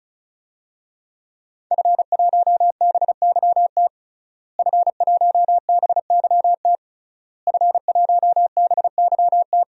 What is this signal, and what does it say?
Morse code, heard as a keyed tone, about 36 wpm, 700 Hz. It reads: F1BYT F1BYT F1BYT
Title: DXCC-100 Call Signs - 5 Characters - Repeated 3x - 35wpm